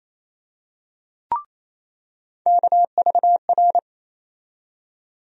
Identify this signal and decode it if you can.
Morse code, heard as a keyed tone, about 28 wpm, 700 Hz. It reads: KVR
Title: USA Call Sign Postfixes - 3 Letters 28wpm